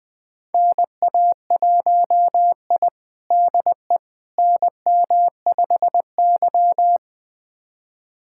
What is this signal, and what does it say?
Morse code, heard as a keyed tone, about 20 wpm, 700 Hz. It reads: NA1I DE NM5Y
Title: USA Callsign DE Callsign - 4 Characters - 20wpm